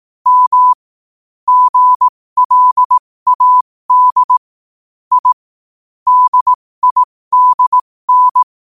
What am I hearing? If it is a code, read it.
Morse code, heard as a keyed tone, about 18 wpm, spaced as for 15 wpm, 1000 Hz. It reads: M GLAD I DIDN
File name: qotdfiles / 15wpm